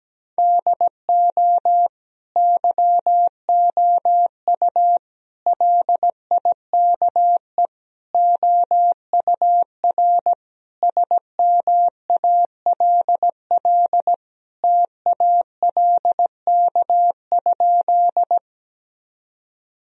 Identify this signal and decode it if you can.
Morse code, heard as a keyed tone, about 17 wpm, 700 Hz. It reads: DO YOU LIKE OUR SMALL TALK?
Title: Sentences from Top 1000 Words - Review Part 1 - 17wpm